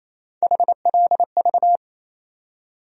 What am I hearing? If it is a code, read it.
Morse code, heard as a keyed tone, about 28 wpm, 700 Hz. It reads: HLV
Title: USA Call Sign Postfixes - 3 Letters 28wpm